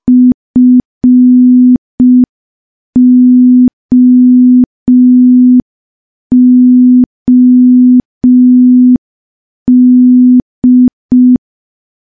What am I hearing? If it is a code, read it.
Morse code, heard as a keyed tone, about 5 wpm, 250 Hz. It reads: FOOD
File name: morse / morse-training-05WPM